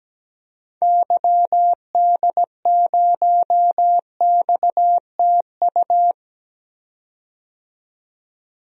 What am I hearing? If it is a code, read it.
Morse code, heard as a keyed tone, about 17 wpm, 700 Hz. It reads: YD0XTU
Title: DXCC-100 Call Signs - 6 Characters - 17wpm